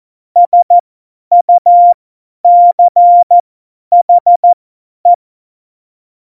Morse code, heard as a keyed tone, about 14 wpm, spaced as for 10 wpm, 700 Hz. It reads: SUCHE
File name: Morsekurs / Witze / Witz 29. Januar 2019, 50 F70